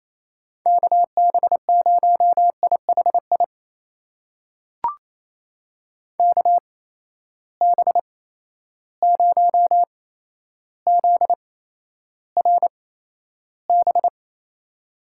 Morse code, heard as a keyed tone, about 28 wpm, spaced as for 12 wpm, 700 Hz. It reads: KB0IHI KB0ZRB
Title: USA Call Signs - 6 Characters - 8x Character Spacing - 28wpm